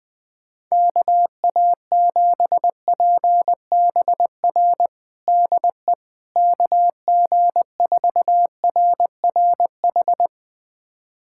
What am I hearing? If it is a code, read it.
Morse code, heard as a keyed tone, about 20 wpm, 700 Hz. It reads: KA7PBR DE KG4RRH
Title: USA Callsign DE Callsign - 6 Characters - 20wpm